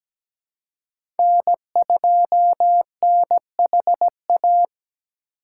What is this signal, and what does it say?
Morse code, heard as a keyed tone, about 17 wpm, 700 Hz. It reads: N2NHA
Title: USA Call Signs - 3x Character Spacing - 17wpm